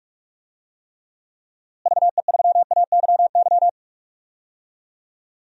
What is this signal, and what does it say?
Morse code, heard as a keyed tone, about 43 wpm, 700 Hz. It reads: VE3AYY
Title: DXCC-100 Call Signs - 6 Characters - Repeated 3x - 45wpm